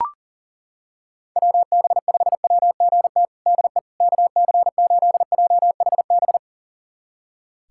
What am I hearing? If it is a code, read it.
Morse code, heard as a keyed tone, about 40 wpm, 700 Hz. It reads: WB5WGT DE KC8JHB
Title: USA Callsign DE Callsign 40wpm